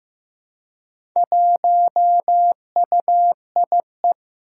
Morse code, heard as a keyed tone, about 15 wpm, 700 Hz. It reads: 1UIE
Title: Four Challenging Letter-Number Combinations 15wpm